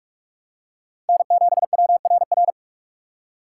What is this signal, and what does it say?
Morse code, heard as a keyed tone, about 46 wpm, 700 Hz. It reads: N7WRR
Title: USA Call Signs - 5 Characters - 4x Character Spacing - 45wpm